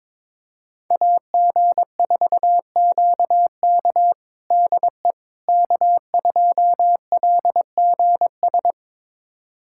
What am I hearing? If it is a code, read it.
Morse code, heard as a keyed tone, about 21 wpm, 700 Hz. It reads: AG4QK DE K2LGS
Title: USA Callsign DE Callsign - 5 Characters - 22wpm